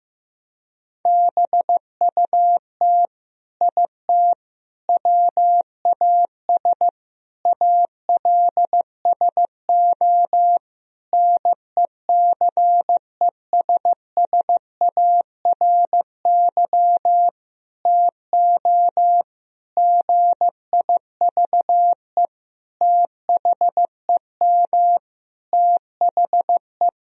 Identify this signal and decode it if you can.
Morse code, heard as a keyed tone, about 15 wpm, 700 Hz. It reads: BUT IT WAS ALSO NECESSARY TO GIVE THEM THE
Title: Sentences from Top 1000 Words - Review Part 2 - 15wpm